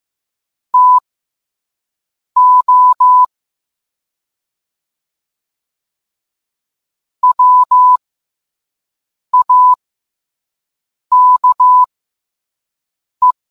Morse code, heard as a keyed tone, about 15 wpm, spaced as for 5 wpm, 1000 Hz. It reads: TO WAKE